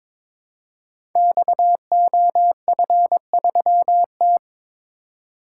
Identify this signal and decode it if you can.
Morse code, heard as a keyed tone, about 22 wpm, 700 Hz. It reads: XOF3T
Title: Code Groups - 5 Letter:Numbers - 22wpm